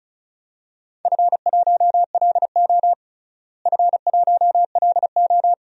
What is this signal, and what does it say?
Morse code, heard as a keyed tone, about 36 wpm, 700 Hz. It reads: F1LO F1LO
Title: DXCC-100 Call Signs - 4 Characters - Repeated 2x - 35wpm